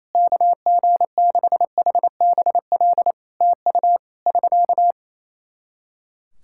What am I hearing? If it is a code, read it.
Morse code, heard as a keyed tone, about 28 wpm, 700 Hz. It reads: KG6HBL TU <SK>
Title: Practice QSO - Exchange 4 - Ending - 28wpm